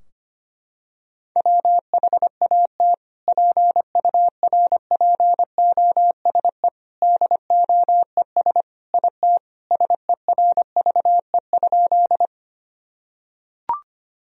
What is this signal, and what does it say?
Morse code, heard as a keyed tone, about 25 wpm, 700 Hz. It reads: WHAT PURPOSE DOES IT SERVE?
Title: Sentences from Top 1000 Words - Review Part 2 - 25wpm